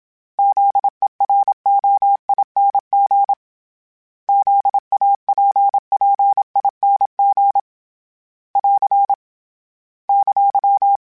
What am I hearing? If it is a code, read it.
Morse code, heard as a keyed tone, about 27 wpm, 800 Hz. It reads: ZEROING ZAPPING <AR> !
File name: Special INGs-mp3s / Special INGs-mp3s / Sngl Ing @25